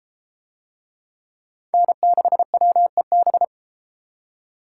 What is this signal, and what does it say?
Morse code, heard as a keyed tone, about 33 wpm, 700 Hz. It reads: N6WEB